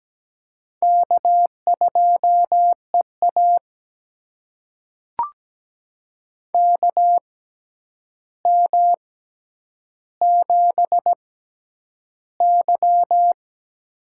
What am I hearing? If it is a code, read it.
Morse code, heard as a keyed tone, about 17 wpm, spaced as for 9 wpm, 700 Hz. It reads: K2EA KM7Y
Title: USA Call Signs - 4 Characters - 6x Character Spacing - 17wpm